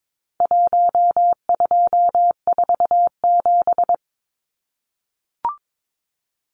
Morse code, heard as a keyed tone, about 22 wpm, 700 Hz. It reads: 1247
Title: Numbers - Contest Style - 4-Digit 22wpm